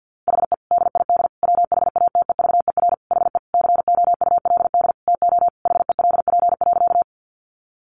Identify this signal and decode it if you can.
Morse code, heard as a keyed tone, about 50 wpm, 700 Hz. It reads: HE DID WHATEVER HE COULD TO HELP.
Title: Sentences from Top 1100 Words 50wpm